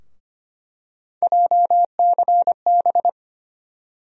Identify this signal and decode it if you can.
Morse code, heard as a keyed tone, about 25 wpm, 700 Hz. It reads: JCB
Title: USA Call Sign Postfixes - 3 Letters 25wpm